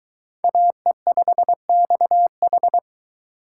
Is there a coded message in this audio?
Yes: Morse code from 0.4 to 2.8 s, heard as a keyed tone, about 23 wpm, 700 Hz. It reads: AE5XH